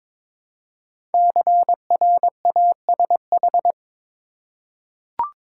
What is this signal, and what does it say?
Morse code, heard as a keyed tone, about 22 wpm, 700 Hz. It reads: CRASH
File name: 5 Letter Words - First 3 Letters Given - 22wpm